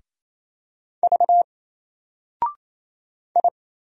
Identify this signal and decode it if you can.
Morse code, heard as a keyed tone, about 27 wpm, 700 Hz. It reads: V I
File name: Single-Letter-Number-28wpm